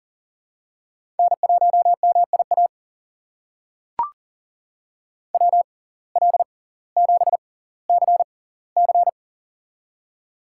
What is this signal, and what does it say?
Morse code, heard as a keyed tone, about 40 wpm, spaced as for 20 wpm, 700 Hz. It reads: N1MIA WL7CC
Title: USA Call Signs - 5 Characters - 6x Character Spacing - 40wpm